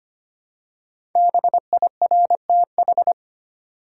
Morse code, heard as a keyed tone, about 25 wpm, 700 Hz. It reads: BIRTH
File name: 5 Letter Words - First 3 Letters Given - 25wpm